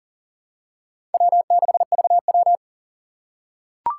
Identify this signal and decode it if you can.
Morse code, heard as a keyed tone, about 40 wpm, 700 Hz. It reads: W6VW
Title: USA Call Signs - 2x Character Spacing - 40wpm